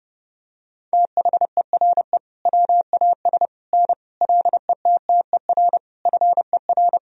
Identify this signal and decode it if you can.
Morse code, heard as a keyed tone, about 32 wpm, 700 Hz. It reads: THERE WAS N LETTER FER
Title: Sentences from Top 300 Words and Abbreviations 30wpm